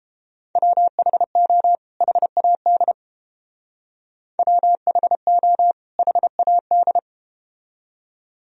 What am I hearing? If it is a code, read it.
Morse code, heard as a keyed tone, about 31 wpm, 700 Hz. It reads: WHO HAD WHO HAD